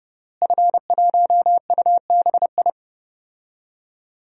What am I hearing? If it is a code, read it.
Morse code, heard as a keyed tone, about 32 wpm, 700 Hz. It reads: F1UBI
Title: DXCC-100 Call Signs - 5 Characters - Repeated 3x - 30wpm